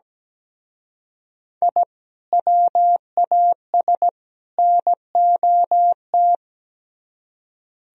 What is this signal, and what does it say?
Morse code, heard as a keyed tone, about 20 wpm, 700 Hz. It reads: I WAS NOT
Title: Speed-Racing - Sets of 3 Words 17wpm